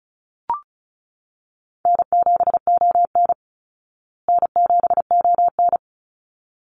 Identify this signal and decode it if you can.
Morse code, heard as a keyed tone, about 34 wpm, 700 Hz. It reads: N7ON N7ON
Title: Call Signs Repeated 3x - 35wpm